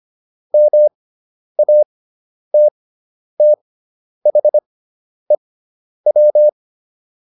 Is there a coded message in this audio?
Yes: Morse code from 0.5 to 6.5 s, heard as a keyed tone, about 25 wpm, spaced as for 10 wpm, 600 Hz. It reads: MATTHEW